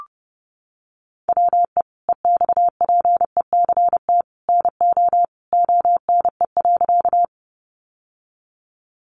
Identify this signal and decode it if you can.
Morse code, heard as a keyed tone, about 29 wpm, 700 Hz. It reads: WE EXPECT NO ONE.
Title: Sentences from Top 1100 Words 30wpm